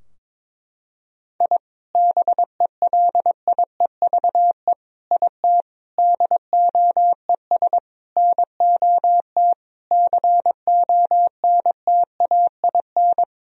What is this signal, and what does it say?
Morse code, heard as a keyed tone, about 22 wpm, 700 Hz. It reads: I BELIEVE IT DOES NOT CONTAIN
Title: Sentences from Top 1000 Words - Review Part 2 - 22wpm